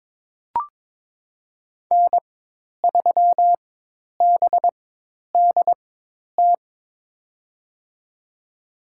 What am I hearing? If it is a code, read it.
Morse code, heard as a keyed tone, about 22 wpm, spaced as for 10 wpm, 700 Hz. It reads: N3BDT
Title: USA Call Signs - 5 Characters - 4x Character Spacing - 22wpm